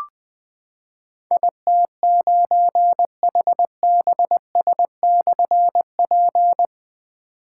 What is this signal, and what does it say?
Morse code, heard as a keyed tone, about 20 wpm, 700 Hz. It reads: IT9HBS/P
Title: Difficult Call Signs 20wpm